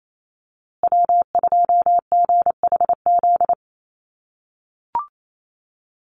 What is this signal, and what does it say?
Morse code, heard as a keyed tone, about 28 wpm, 700 Hz. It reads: W2GHZ
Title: USA Call Signs Repeated 2x 28wpm